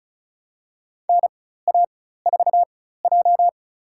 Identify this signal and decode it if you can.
Morse code, heard as a keyed tone, about 36 wpm, spaced as for 16 wpm, 700 Hz. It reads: NA4J
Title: USA Call Signs - 4 Characters - 4x Character Spacing - 35wpm